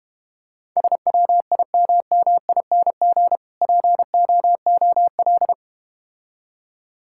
Morse code, heard as a keyed tone, about 31 wpm, 700 Hz. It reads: SWIMMING POOL
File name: Easy Words - Speed-Racing with Hints - 25wpm